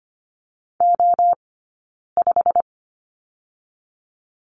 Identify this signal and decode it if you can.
Morse code, heard as a keyed tone, about 25 wpm, 700 Hz. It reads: O 5
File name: Lesson 15d - Number 5 - Two Characters 25wpm